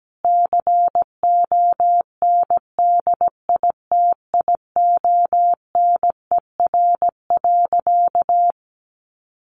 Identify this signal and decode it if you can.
Morse code, heard as a keyed tone, about 17 wpm, 700 Hz. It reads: CONDITIONER.
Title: Easy Phrases 17wpm